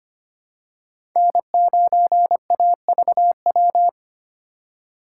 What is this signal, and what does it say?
Morse code, heard as a keyed tone, about 25 wpm, 700 Hz. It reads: N9AVW
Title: USA Call Signs - 5 Characters - 3x Character Spacing - 25wpm